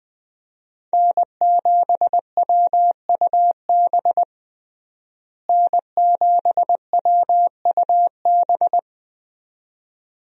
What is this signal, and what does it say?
Morse code, heard as a keyed tone, about 20 wpm, 700 Hz. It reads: N7WUB N7WUB
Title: USA Call Signs - 5 Characters - Repeated 2x - 20wpm